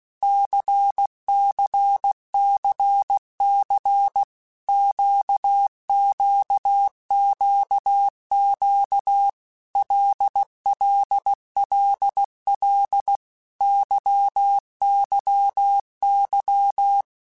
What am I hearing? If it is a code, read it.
Morse code, heard as a keyed tone, about 16 wpm, 800 Hz. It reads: CCCC QQQQ LLLL YYY
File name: Basic Letter Groups By7Groups / LFCYQ / LFCYQ@15